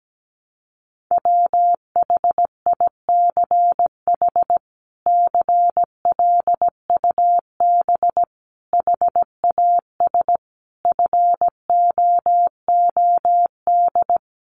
Morse code, heard as a keyed tone, about 17 wpm, 700 Hz. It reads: WHICH CLUB HAS FOOD
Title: Sentences from Top 1200 Words 17wpm